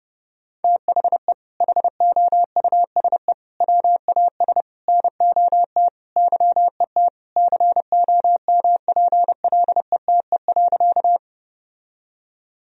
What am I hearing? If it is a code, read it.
Morse code, heard as a keyed tone, about 30 wpm, 700 Hz. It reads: THE HOUSE WAS NOT YET COMPLETE.
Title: Sentences from Top 1000 Words - Review Part 2 - 30wpm